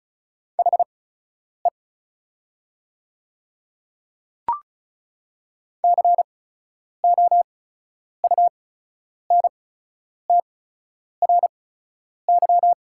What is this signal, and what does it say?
Morse code, heard as a keyed tone, about 36 wpm, spaced as for 10 wpm, 700 Hz. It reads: HE COUNTRY
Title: 8x Character Spacing - Top 300 Words - 35wpm